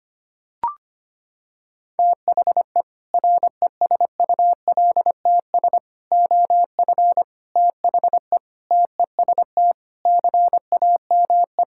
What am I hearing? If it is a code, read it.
Morse code, heard as a keyed tone, about 25 wpm, 700 Hz. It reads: THE RESULTS OF THE TEST CAME
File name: practice / 25wpm / Sentences with 4-Word Phrases - 25wpm